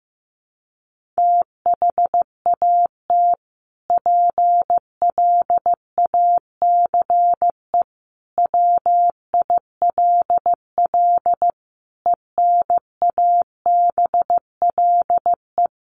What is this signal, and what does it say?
Morse code, heard as a keyed tone, about 15 wpm, 700 Hz. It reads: THAT PLACE WILL ENABLE
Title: Sentences from Top 1500 Words 15wpm